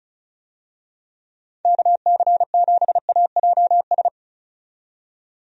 Morse code, heard as a keyed tone, about 34 wpm, 700 Hz. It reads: KC7AJS
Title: USA Call Signs - 6 Characters - 35wpm